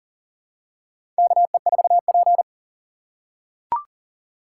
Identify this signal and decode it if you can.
Morse code, heard as a keyed tone, about 40 wpm, 700 Hz. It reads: KE4P
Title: USA Call Signs - 3x Character Spacing - 40wpm